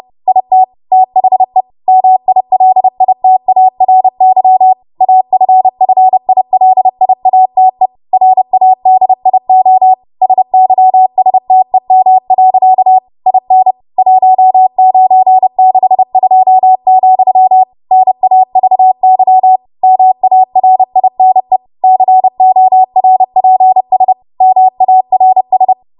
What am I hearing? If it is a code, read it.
Morse code, heard as a keyed tone, about 30 wpm, 750 Hz. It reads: IT THE MILITARY AFFILIATE RADIO SYSTEM. IN 1962, NAVY MARINE CORPS MARS
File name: Archive / 30 WPM / 30WPM